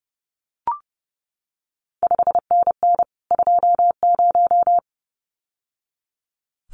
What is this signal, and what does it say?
Morse code, heard as a keyed tone, about 30 wpm, 700 Hz. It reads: 5NN 20
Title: CQ WW DX Contest Exchange - 30wpm